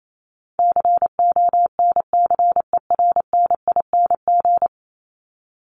Morse code, heard as a keyed tone, about 28 wpm, 700 Hz. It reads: CONCERNING
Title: Top 1751-2000 Words 28wpm